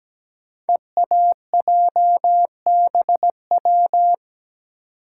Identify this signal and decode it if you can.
Morse code, heard as a keyed tone, about 17 wpm, 700 Hz. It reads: EAJBW
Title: Code Groups - 5 Letters - 17wpm